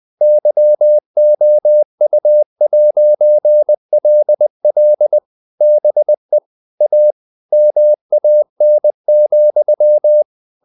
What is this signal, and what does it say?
Morse code, heard as a keyed tone, about 20 wpm, 600 Hz. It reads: YOU'LL BE A MAN,
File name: CW / Practice / If20